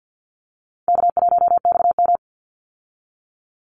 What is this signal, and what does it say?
Morse code, heard as a keyed tone, about 50 wpm, 700 Hz. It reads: K1XM
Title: Easy Call Signs Repeated 3x - 50wpm